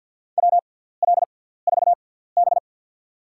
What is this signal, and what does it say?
Morse code, heard as a keyed tone, about 47 wpm, spaced as for 17 wpm, 700 Hz. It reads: WL4B
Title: USA Call Signs - 6x Character Spacing - 50wpm